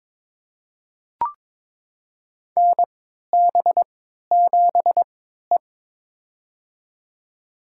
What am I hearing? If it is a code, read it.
Morse code, heard as a keyed tone, about 22 wpm, spaced as for 13 wpm, 700 Hz. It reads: NB7E
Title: USA Call Signs - 4 Characters - 3x Character Spacing - 22wpm